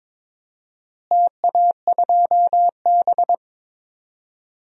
Morse code, heard as a keyed tone, about 22 wpm, 700 Hz. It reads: TA2B
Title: DXCC-100 Call Signs - 4 Characters - Repeated 2x - 22wpm